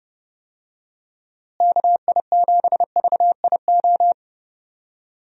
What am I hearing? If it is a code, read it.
Morse code, heard as a keyed tone, about 32 wpm, 700 Hz. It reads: KI7VIO
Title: USA Call Signs - 6 Characters - Repeated 3x - 30wpm